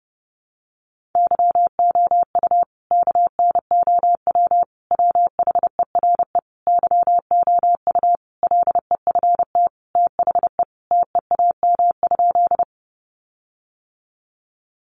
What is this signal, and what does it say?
Morse code, heard as a keyed tone, about 30 wpm, 700 Hz. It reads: YOU KNOW WHERE YOU LEFT THE TEAM?
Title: Sentences from Top 1500 Words 30wpm